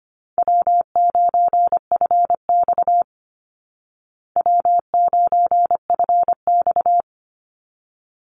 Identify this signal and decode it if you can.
Morse code, heard as a keyed tone, about 25 wpm, 700 Hz. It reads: W9FX W9FX
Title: Call Signs Repeated 3x - 25wpm